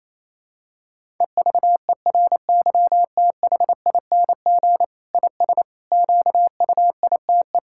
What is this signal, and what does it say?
Morse code, heard as a keyed tone, about 28 wpm, 700 Hz. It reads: EVERYTHING IS QUITE
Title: Sentences from Top 1000 Words - Review Part 6 - 28wpm